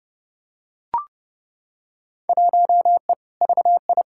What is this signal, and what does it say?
Morse code, heard as a keyed tone, about 30 wpm, 700 Hz. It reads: 1E VI
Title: Field Day Exchanges Encore - 30wpm